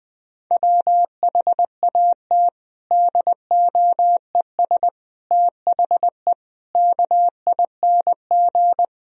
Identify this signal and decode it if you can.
Morse code, heard as a keyed tone, about 20 wpm, 700 Hz. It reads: WHAT DOES THE KING
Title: Sentences from Top 1000 Words - Review Part 2 - 20wpm